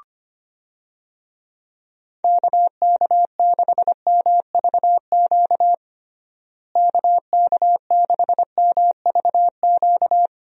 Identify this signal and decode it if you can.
Morse code, heard as a keyed tone, about 25 wpm, 700 Hz. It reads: KK6MVQ KK6MVQ
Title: USA Call Signs - 6 Characters - Repeated 3x - 25wpm